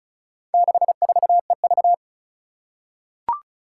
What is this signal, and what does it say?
Morse code, heard as a keyed tone, about 36 wpm, 700 Hz. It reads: 64EV
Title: Four Challenging Letter-Number Combinations 35wpm